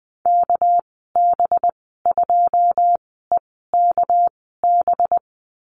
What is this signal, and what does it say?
Morse code, heard as a keyed tone, about 20 wpm, spaced as for 15 wpm, 700 Hz. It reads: KB2EKB
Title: 219 Lesson 35g - Number 0 - Call Signs 20wpm